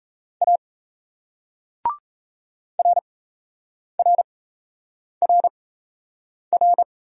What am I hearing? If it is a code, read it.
Morse code, heard as a keyed tone, about 40 wpm, spaced as for 8 wpm, 700 Hz. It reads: A RRRR